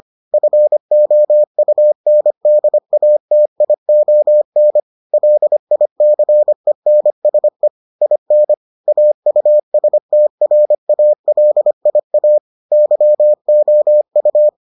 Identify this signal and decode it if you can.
Morse code, heard as a keyed tone, about 25 wpm, 600 Hz. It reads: FOUNDATION LICENSE IN AUSTRALIA YOU